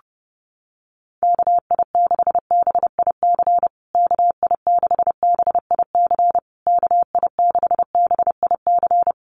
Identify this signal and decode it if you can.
Morse code, heard as a keyed tone, about 30 wpm, 700 Hz. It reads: KI6BIC KI6BIC KI6BIC
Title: USA Call Signs Repeated 3x 30wpm